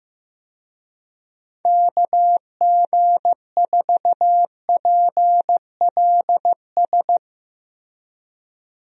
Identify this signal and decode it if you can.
Morse code, heard as a keyed tone, about 15 wpm, 700 Hz. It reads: KG4PLS
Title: USA Call Signs - 6 Characters - Repeated 3x - 15wpm